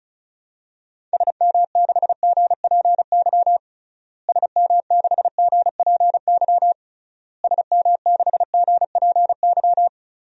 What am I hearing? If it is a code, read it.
Morse code, heard as a keyed tone, about 36 wpm, 700 Hz. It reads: SM6GPY SM6GPY SM6GPY
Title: DXCC-100 Call Signs - 6 Characters - Repeated 3x - 35wpm